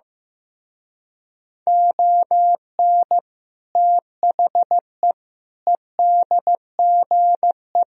Morse code, heard as a keyed tone, about 15 wpm, 700 Hz. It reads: ON THE EDGE
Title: Speed-Racing - Sets of 4 Words 15wpm